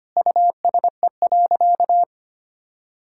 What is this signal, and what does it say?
Morse code, heard as a keyed tone, about 25 wpm, 700 Hz. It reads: USE.
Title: Sentences from Top 1000 Words - Review Part 6 - 25wpm